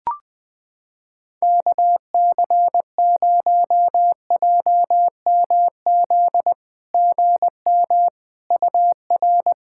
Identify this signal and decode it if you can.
Morse code, heard as a keyed tone, about 20 wpm, 690 Hz. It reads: KC0JMZ GM UR
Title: SOTA QSO Practice 20wpm